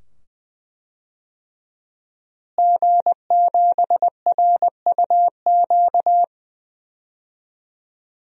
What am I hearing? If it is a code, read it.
Morse code, heard as a keyed tone, about 20 wpm, 700 Hz. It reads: G7RUQ
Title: DXCC-100 Call Signs - 5 Characters - Repeated 2x - 20wpm